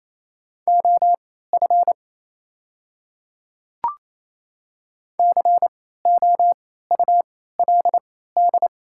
Morse code, heard as a keyed tone, about 28 wpm, spaced as for 16 wpm, 700 Hz. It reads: OF COULD
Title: 3x Character Spacing - Top 100 Words - 28wpm